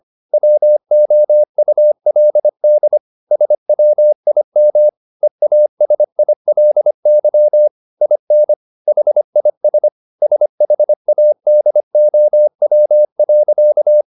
Morse code, heard as a keyed tone, about 25 wpm, 600 Hz. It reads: WOULD SWIM EASILY IN HIS SHADOW.